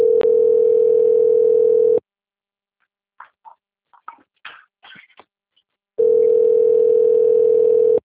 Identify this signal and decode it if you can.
Ringback tone (North America), 440 + 480 Hz.